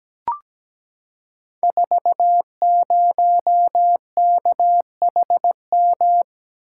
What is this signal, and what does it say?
Morse code, heard as a keyed tone, about 17 wpm, 700 Hz. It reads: 40KHM
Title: Code Groups - 5 Letter:Numbers - 17wpm